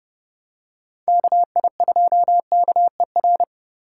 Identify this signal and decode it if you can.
Morse code, heard as a keyed tone, about 29 wpm, 700 Hz. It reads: KI2KER